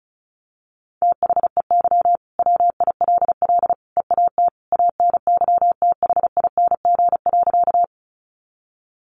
Morse code, heard as a keyed tone, about 34 wpm, 700 Hz. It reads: THEY WILL EAT ANYTHING.
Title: Sentences from Top 1000 Words 35wpm